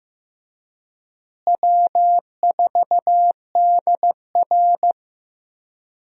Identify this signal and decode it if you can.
Morse code, heard as a keyed tone, about 15 wpm, 700 Hz. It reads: W4DR
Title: USA Call Signs - 4 Characters - Repeated 2x - 15wpm